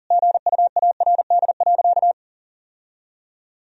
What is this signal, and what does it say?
Morse code, heard as a keyed tone, about 40 wpm, 700 Hz. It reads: GUARD.
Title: Sentences from Top 1000 Words - Review Part 4 - 40wpm